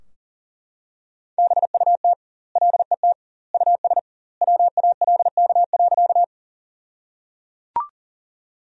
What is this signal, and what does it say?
Morse code, heard as a keyed tone, about 37 wpm, 700 Hz. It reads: BUT LET US WALK.
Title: 2x Word-Spacing - Sentences from Top 300 Words - 40wpm